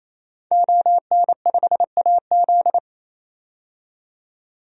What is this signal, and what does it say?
Morse code, heard as a keyed tone, about 28 wpm, 700 Hz. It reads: ON5AZ
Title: DXCC-100 Call Signs - 5 Characters - 28wpm